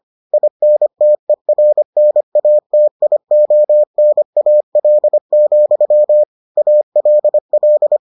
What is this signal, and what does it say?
Morse code, heard as a keyed tone, about 25 wpm, 600 Hz. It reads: INTERNATIONAL, ALL